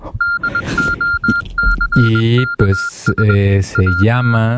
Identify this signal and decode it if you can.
Morse code, heard as a keyed tone, about 21 wpm, 1400 Hz, with speech. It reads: DOCKS AT